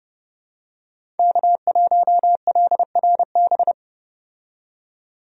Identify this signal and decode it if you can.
Morse code, heard as a keyed tone, about 30 wpm, 700 Hz. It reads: K1LRB
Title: USA Call Signs - 5 Characters - 30wpm